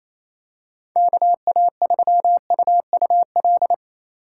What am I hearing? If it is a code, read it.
Morse code, heard as a keyed tone, about 28 wpm, 700 Hz. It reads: KA3UUL
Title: USA Call Signs - 6 Characters - 2x Character Spacing - 28wpm